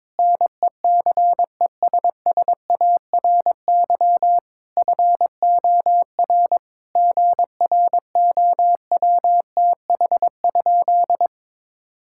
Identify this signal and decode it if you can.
Morse code, heard as a keyed tone, about 22 wpm, 700 Hz. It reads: NECESSARY FOR GROWTH?
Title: Sentences from Top 1000 Words - Review Part 5 - 22wpm